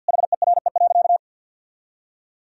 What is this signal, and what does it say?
Morse code, heard as a keyed tone, about 52 wpm, 700 Hz. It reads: HERE.
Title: Sentences from Top 1000 Words - Review Part 4 - 50wpm